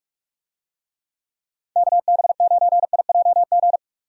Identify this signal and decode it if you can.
Morse code, heard as a keyed tone, about 49 wpm, 700 Hz. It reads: KB9IJG